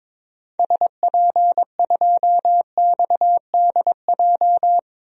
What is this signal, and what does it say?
Morse code, heard as a keyed tone, about 22 wpm, 700 Hz. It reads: SP2XDJ
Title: DXCC-100 Call Signs - 6 Characters - 22wpm